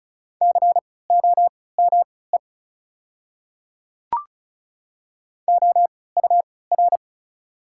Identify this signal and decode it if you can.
Morse code, heard as a keyed tone, about 36 wpm, spaced as for 20 wpm, 690 Hz. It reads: COME OUR
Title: 3x Character Spacing - Top 500 Words - 35wpm